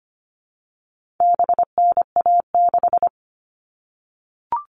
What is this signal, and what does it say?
Morse code, heard as a keyed tone, about 25 wpm, 700 Hz. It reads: BNA6
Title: All Cut Numbers - 4-Digit 25wpm